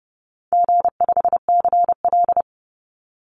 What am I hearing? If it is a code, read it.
Morse code, heard as a keyed tone, about 30 wpm, 700 Hz. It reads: G5CL
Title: Call Signs Repeated 2x - 30wpm